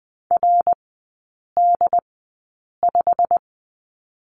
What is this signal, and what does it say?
Morse code, heard as a keyed tone, about 20 wpm, spaced as for 8 wpm, 700 Hz. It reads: RD5
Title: 121 Lesson 20e - Number 3 - Three Characters 20wpm